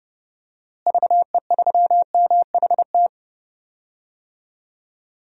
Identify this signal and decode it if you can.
Morse code, heard as a keyed tone, about 30 wpm, 700 Hz. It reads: VE3MHT
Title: DXCC-100 Call Signs - 6 Characters - Repeated 3x - 30wpm